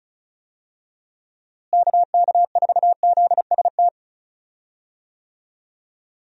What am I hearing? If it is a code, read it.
Morse code, heard as a keyed tone, about 36 wpm, 700 Hz. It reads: KK4ZST
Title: USA Call Signs - 6 Characters - Repeated 3x - 35wpm